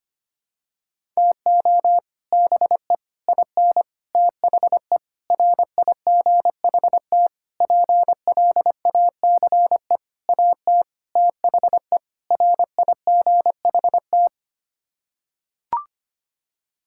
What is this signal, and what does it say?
Morse code, heard as a keyed tone, about 25 wpm, 700 Hz. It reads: TO BE IN THE RIGHT PLACE AT THE RIGHT
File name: Sets of 9 Words - 25wpm